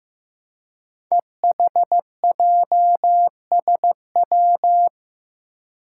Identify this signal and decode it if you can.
Morse code, heard as a keyed tone, about 15 wpm, 700 Hz. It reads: EHJSW
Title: Code Groups - 5 Letters - 15wpm